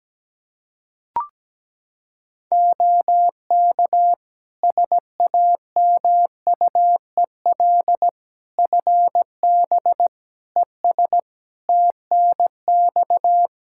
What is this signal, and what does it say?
Morse code, heard as a keyed tone, about 17 wpm, 700 Hz. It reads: OK SAMUEL FB ES TNX
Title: Practice QSO - Exchange 2 - Info - 17wpm